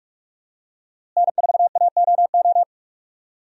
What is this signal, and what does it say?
Morse code, heard as a keyed tone, about 46 wpm, 700 Hz. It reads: N4AOO
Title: USA Call Signs - 5 Characters - Repeated 2x - 45wpm